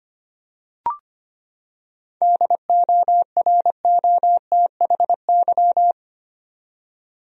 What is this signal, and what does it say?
Morse code, heard as a keyed tone, about 25 wpm, 700 Hz. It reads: DOROTHY
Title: USA Common Names - 25wpm